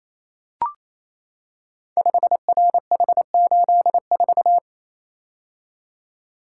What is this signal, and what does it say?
Morse code, heard as a keyed tone, about 27 wpm, 700 Hz. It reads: 5RH84
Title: Code Groups - 5 Letter:Numbers - 28wpm